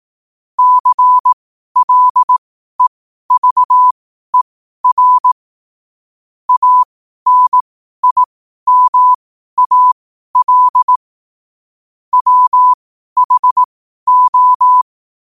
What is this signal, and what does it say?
Morse code, heard as a keyed tone, about 18 wpm, spaced as for 12 wpm, 1000 Hz. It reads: CLEVER ANIMAL WHO
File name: qotdfiles / 12wpm